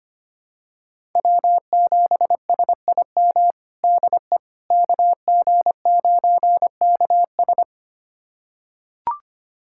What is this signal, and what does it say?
Morse code, heard as a keyed tone, about 25 wpm, 700 Hz. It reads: W7SIM DE KG9KS
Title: USA Callsign DE Callsign - 5 Characters - 25wpm